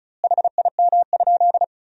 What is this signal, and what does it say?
Morse code, heard as a keyed tone, about 36 wpm, 700 Hz. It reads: HIM?
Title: 2x Word-Spacing - Sentences from Top 200 Words - 35wpm